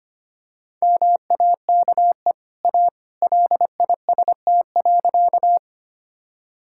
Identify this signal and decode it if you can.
Morse code, heard as a keyed tone, about 25 wpm, 690 Hz. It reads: MAKE A LIST.
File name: Easy Sentences with Hints - 25wpm